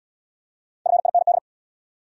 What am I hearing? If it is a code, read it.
Morse code, heard as a keyed tone, about 69 wpm, 700 Hz. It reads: FIND